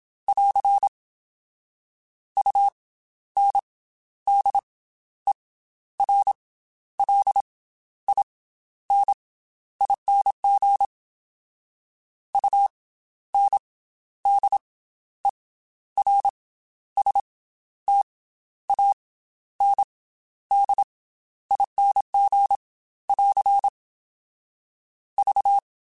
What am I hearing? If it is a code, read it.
Morse code, heard as a keyed tone, about 27 wpm, spaced as for 12 wpm, 800 Hz. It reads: <AR> UNDERLINING UNDERSTANDING <AR> V
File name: Special INGs / Special INGs-mp3s / Slow ing @25